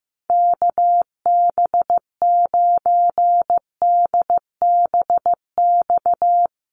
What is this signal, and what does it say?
Morse code, heard as a keyed tone, about 15 wpm, 700 Hz. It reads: KB9DBX
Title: USA Call Signs Repeated 3x 15wpm